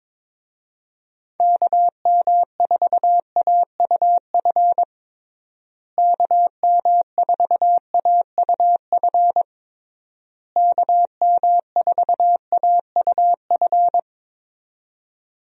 Morse code, heard as a keyed tone, about 22 wpm, 700 Hz. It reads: KM4AUF KM4AUF KM4AUF
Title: USA Call Signs - 6 Characters - Repeated 3x - 22wpm